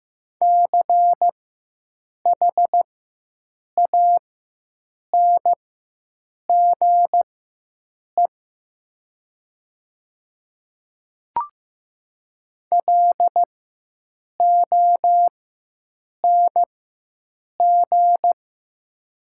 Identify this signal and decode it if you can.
Morse code, heard as a keyed tone, about 15 wpm, spaced as for 7 wpm, 700 Hz. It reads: CHANGE LONG
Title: 4x Character Spacing - Top 200 Words - 15wpm